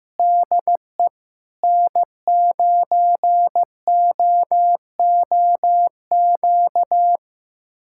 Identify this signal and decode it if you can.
Morse code, heard as a keyed tone, about 16 wpm, 700 Hz. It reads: DE N9OOQ
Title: USA Callsign DE Callsign 15wpm